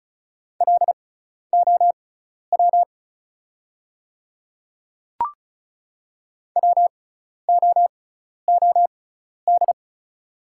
Morse code, heard as a keyed tone, about 36 wpm, spaced as for 12 wpm, 690 Hz. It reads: LOW WOOD
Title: 6x Character Spacing - Top 300 Words - 35wpm